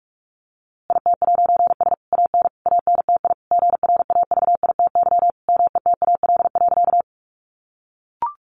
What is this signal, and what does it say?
Morse code, heard as a keyed tone, about 44 wpm, 700 Hz. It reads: IT'S AN ANTI GRAVITY METAL.
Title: A Sound Of Thunder 45wpm